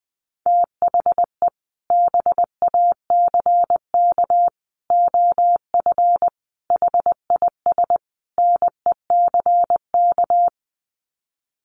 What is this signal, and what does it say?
Morse code, heard as a keyed tone, about 19 wpm, 700 Hz. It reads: THE BACK OF HIS NECK
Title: Sets of 5 Words Encore 20wpm